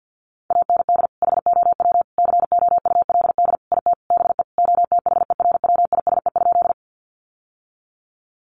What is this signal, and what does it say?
Morse code, heard as a keyed tone, about 48 wpm, 700 Hz. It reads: AND HOW COULD IT BE OTHERWISE?
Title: Sentences from Top 1200 Words 50wpm